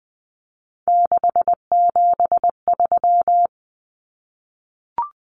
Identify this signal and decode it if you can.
Morse code, heard as a keyed tone, about 20 wpm, 700 Hz. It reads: 673
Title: Common Cut Numbers - 3-Digit 20wpm